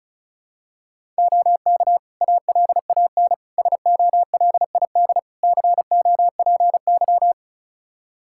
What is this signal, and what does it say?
Morse code, heard as a keyed tone, about 36 wpm, 700 Hz. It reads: OK ALAN SOLID COPY
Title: Practice QSO - Exchange 3 - More Info - 35wpm